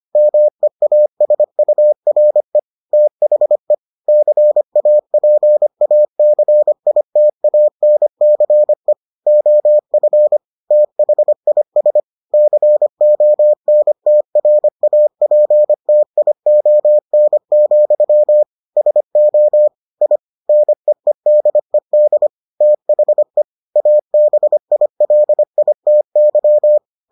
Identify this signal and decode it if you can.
Morse code, heard as a keyed tone, about 25 wpm, 600 Hz. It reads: MEASURE THE CAPACITANCE OF THIS CONTRAPTION, SO I NEEDED THE ABILITY